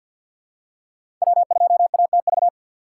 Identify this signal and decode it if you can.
Morse code, heard as a keyed tone, about 52 wpm, 700 Hz. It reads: W2ATV
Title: USA Call Signs - 5 Characters - 2x Character Spacing - 50wpm